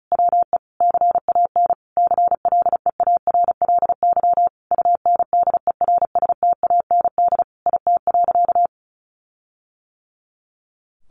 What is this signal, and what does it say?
Morse code, heard as a keyed tone, about 34 wpm, 700 Hz. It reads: WE CAN CLEARLY UNDERSTAND IT.
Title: Sentences from Top 1400 Words 35wpm